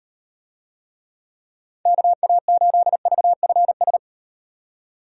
Morse code, heard as a keyed tone, about 38 wpm, 700 Hz. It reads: KA8VFS